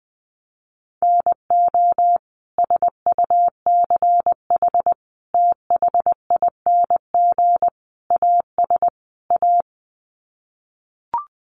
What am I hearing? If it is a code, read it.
Morse code, heard as a keyed tone, about 20 wpm, 700 Hz. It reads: NO SUCH THING AS A
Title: Sets of 5 Words Encore 20wpm